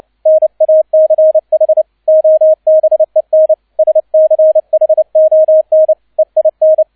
Morse code, heard as a keyed tone, about 29 wpm, 630 Hz, with radio noise. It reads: NACH OBEN SCHON EIN
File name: 002 Zweiter Streich 30wpm